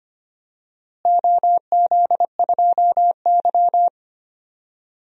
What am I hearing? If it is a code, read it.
Morse code, heard as a keyed tone, about 25 wpm, 700 Hz. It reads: OZ2Y
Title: DXCC-100 Call Signs - 4 Characters - Repeated 3x - 25wpm